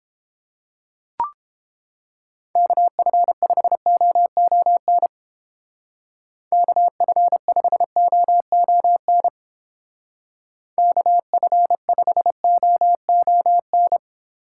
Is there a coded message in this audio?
Yes: Morse code from 2.6 to 14.0 s, heard as a keyed tone, about 31 wpm, 700 Hz. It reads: KF5OON KF5OON KF5OON